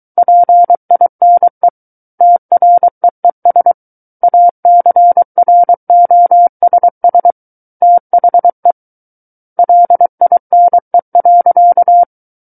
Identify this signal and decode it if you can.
Morse code, heard as a keyed tone, about 23 wpm, 700 Hz. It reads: PINE TREES ACROSS THE LINE.
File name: Book One / 14-Chapter 14 - 23 WPM